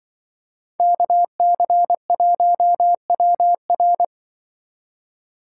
Morse code, heard as a keyed tone, about 25 wpm, 700 Hz. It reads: KC1WR